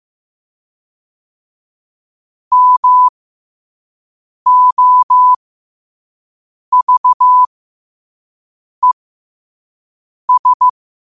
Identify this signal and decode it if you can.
Morse code, heard as a keyed tone, about 15 wpm, spaced as for 5 wpm, 1000 Hz. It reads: MOVES
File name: qotdfiles / 5wpm